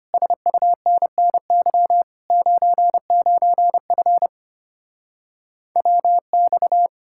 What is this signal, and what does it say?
Morse code, heard as a keyed tone, about 33 wpm, 700 Hz. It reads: SUNNY 99F WX